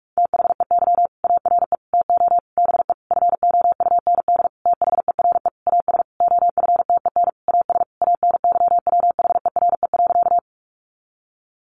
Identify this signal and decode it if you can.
Morse code, heard as a keyed tone, about 44 wpm, 700 Hz. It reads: THEY ARE TO BE FOUND THERE AS OFTEN AS ANYWHERE.
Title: Sentences from Top 1400 Words 45wpm